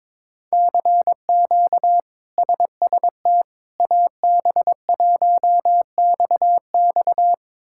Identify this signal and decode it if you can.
Morse code, heard as a keyed tone, about 21 wpm, 700 Hz. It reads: CQ SST AB1XX
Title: K1USN SST QSO Practice - 22wpm